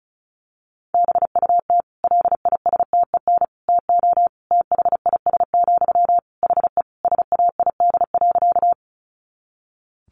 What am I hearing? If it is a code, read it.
Morse code, heard as a keyed tone, about 34 wpm, 700 Hz. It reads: BUT LISTEN TO THIS, HE SAID.
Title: Sentences from Top 400 Words 35wpm